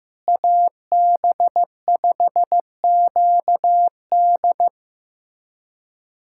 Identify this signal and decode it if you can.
Morse code, heard as a keyed tone, about 15 wpm, 700 Hz. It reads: AB5QD
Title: USA Callsign DE Callsign - 5 Characters - 15wpm